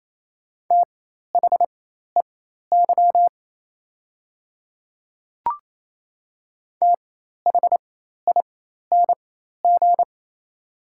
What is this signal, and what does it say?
Morse code, heard as a keyed tone, about 27 wpm, spaced as for 13 wpm, 690 Hz. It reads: THEY THING
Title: 4x Character Spacing - Top 100 Words - 28wpm